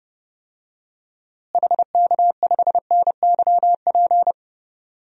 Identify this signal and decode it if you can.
Morse code, heard as a keyed tone, about 30 wpm, 700 Hz. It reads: HK5NYP
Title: DXCC-100 Call Signs - 6 Characters - Repeated 2x - 30wpm